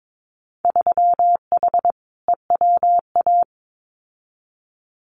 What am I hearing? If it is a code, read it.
Morse code, heard as a keyed tone, about 22 wpm, 700 Hz. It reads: 3H EWA
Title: Winter Field Day Exchanges Encore 22wpm